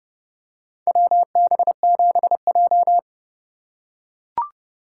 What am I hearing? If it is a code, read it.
Morse code, heard as a keyed tone, about 30 wpm, 700 Hz. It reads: WB7J
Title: USA Call Signs - 4 Characters - 4x Character Spacing - 30wpm